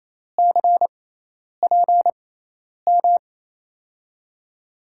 Morse code, heard as a keyed tone, about 28 wpm, spaced as for 10 wpm, 700 Hz. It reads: CPM
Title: USA Call Signs - 6 Characters - 6x Character Spacing - 28wpm